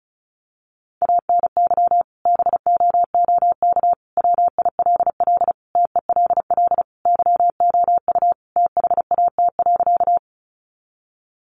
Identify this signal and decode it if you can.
Morse code, heard as a keyed tone, about 34 wpm, 700 Hz. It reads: ANY BOOK WILL TELL YOU THAT.
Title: Sentences from Top 300 Words 35wpm